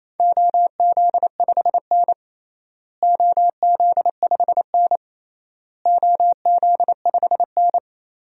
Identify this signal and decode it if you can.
Morse code, heard as a keyed tone, about 28 wpm, 700 Hz. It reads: OZ5N OZ5N OZ5N
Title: DXCC-100 Call Signs - 4 Characters - Repeated 3x - 28wpm